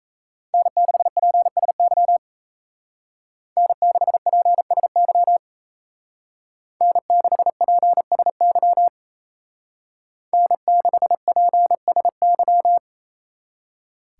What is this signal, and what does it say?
Morse code, heard as a keyed tone, about 36 wpm, 700 Hz. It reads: N6PSY N6PSY N6PSY N6PSY